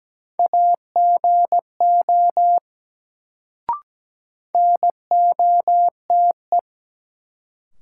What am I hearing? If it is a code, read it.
Morse code, heard as a keyed tone, about 17 wpm, 700 Hz. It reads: AGO NOTE
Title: Sentences from Top 500 Words - New Word Drill - 17wpm